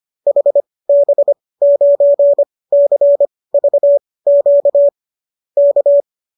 Morse code, heard as a keyed tone, about 25 wpm, spaced as for 18 wpm, 570 Hz. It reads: HB9CVQ K